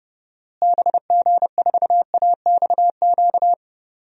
Morse code, heard as a keyed tone, about 31 wpm, 700 Hz. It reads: BG4AXQ
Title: DXCC-100 Call Signs - 6 Characters - Repeated 2x - 30wpm